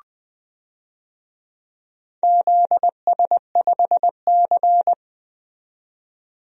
Morse code, heard as a keyed tone, about 20 wpm, 700 Hz. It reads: ZS5C
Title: DXCC-100 Call Signs - 4 Characters - 20wpm